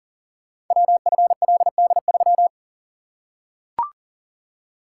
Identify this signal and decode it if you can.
Morse code, heard as a keyed tone, about 42 wpm, 690 Hz. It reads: WFLD3
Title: Code Groups - 5 Letter:Numbers - 40wpm